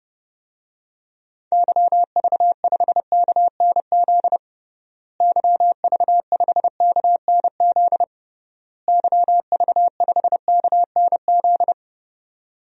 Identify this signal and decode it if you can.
Morse code, heard as a keyed tone, about 32 wpm, 700 Hz. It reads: YV5KNZ YV5KNZ YV5KNZ
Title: DXCC-100 Call Signs - 6 Characters - Repeated 3x - 30wpm